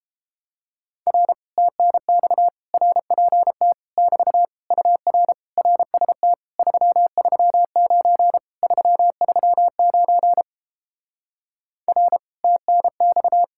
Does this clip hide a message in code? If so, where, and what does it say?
Yes: Morse code from 1.1 to 13.4 s, heard as a keyed tone, about 30 wpm, 700 Hz. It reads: R TNX RPT <BT> UR RST 339 339 R TNX